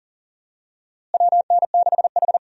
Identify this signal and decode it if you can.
Morse code, heard as a keyed tone, about 39 wpm, 700 Hz. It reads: WN6H
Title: USA Call Signs - 4 Characters - Repeated 3x - 40wpm